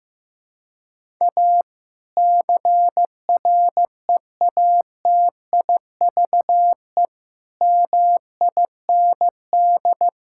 Morse code, heard as a keyed tone, about 16 wpm, 700 Hz. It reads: A CREATIVE MIND
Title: Sentences from Top 1000 Words - Review Part 5 - 15wpm